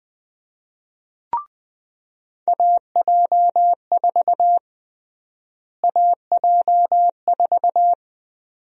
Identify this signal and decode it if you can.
Morse code, heard as a keyed tone, about 20 wpm, 700 Hz. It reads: AJ4 AJ4
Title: USA Call Sign Prefixes - Repeated 2x - 20wpm